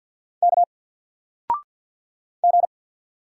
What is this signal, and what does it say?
Morse code, heard as a keyed tone, about 48 wpm, 700 Hz. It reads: K G
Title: Single Letter-Number 50wpm